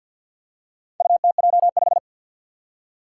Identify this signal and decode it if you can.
Morse code, heard as a keyed tone, about 46 wpm, 700 Hz. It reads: UTJ5
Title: Four Challenging Letter-Number Combinations 50wpm